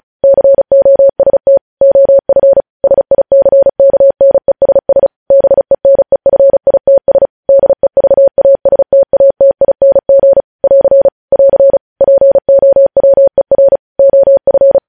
Morse code, heard as a keyed tone, about 36 wpm, 550 Hz. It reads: COST OF SICKNESS BENEFITS DEVASTATING <AR> <AR> POWER OF
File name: podcast / News headlines-1x-35x35